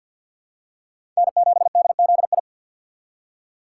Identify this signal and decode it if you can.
Morse code, heard as a keyed tone, about 53 wpm, 690 Hz. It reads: N7DZI
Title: USA Call Signs - 5 Characters - 2x Character Spacing - 50wpm